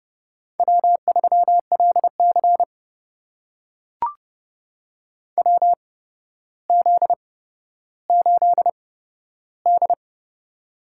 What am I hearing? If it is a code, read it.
Morse code, heard as a keyed tone, about 30 wpm, spaced as for 13 wpm, 700 Hz. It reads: W3LC WZ8D
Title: USA Call Signs - 4 Characters - 8x Character Spacing - 30wpm